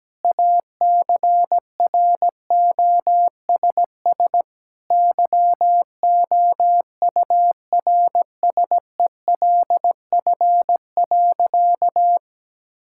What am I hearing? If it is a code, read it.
Morse code, heard as a keyed tone, about 17 wpm, 700 Hz. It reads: ACROSS YOURSELF.
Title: Sentences from Top 1000 Words - Review Part 5 - 17wpm